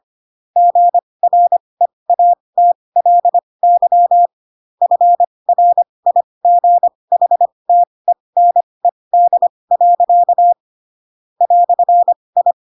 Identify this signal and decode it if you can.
Morse code, heard as a keyed tone, about 25 wpm, spaced as for 20 wpm, 700 Hz. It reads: GREATLY FRIGHTENED. "I
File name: Pinocchio / Pinocchio ch4 15 25 ik1hge